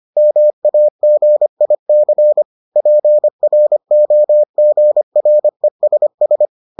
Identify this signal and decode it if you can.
Morse code, heard as a keyed tone, about 25 wpm, 600 Hz. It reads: MAGIC PROGRESS